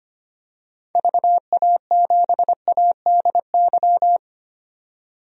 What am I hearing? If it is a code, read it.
Morse code, heard as a keyed tone, about 25 wpm, 700 Hz. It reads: VA7ADY
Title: DXCC-100 Call Signs - 6 Characters - 25wpm